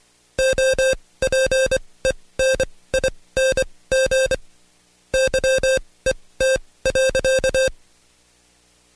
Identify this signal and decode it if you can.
Morse code, heard as a keyed tone, about 24 wpm, spaced as for 18 wpm, 530 Hz. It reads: OPENING YET.
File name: Code Quick / AudioDownld2 / Seg135-Stepper8 17WPM